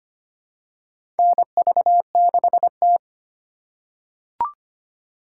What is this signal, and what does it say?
Morse code, heard as a keyed tone, about 25 wpm, 700 Hz. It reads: NV6T
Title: USA Call Signs - 4x Character Spacing - 25wpm